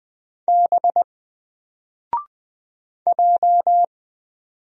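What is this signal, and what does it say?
Morse code, heard as a keyed tone, about 20 wpm, 700 Hz. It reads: B J
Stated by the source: Single letter and number Morse drill at 20 wpm, from N4LCD.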